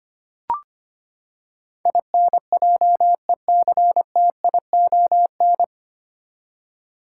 Morse code, heard as a keyed tone, about 25 wpm, 700 Hz. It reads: INJECTION
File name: Nine Letter Words - 25wpm